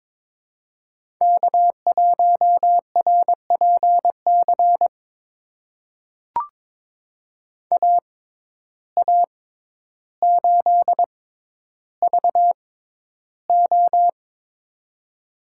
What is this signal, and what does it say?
Morse code, heard as a keyed tone, about 22 wpm, spaced as for 11 wpm, 700 Hz. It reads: K1RPC AA8VO
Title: USA Call Signs - 5 Characters - 6x Character Spacing - 22wpm